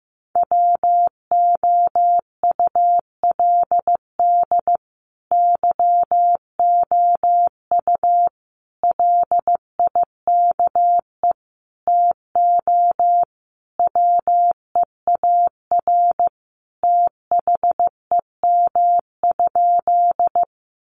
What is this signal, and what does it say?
Morse code, heard as a keyed tone, about 15 wpm, 700 Hz. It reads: WOULD YOU LIKE TO WEAR THEM?
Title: Sentences from Top 1500 Words 15wpm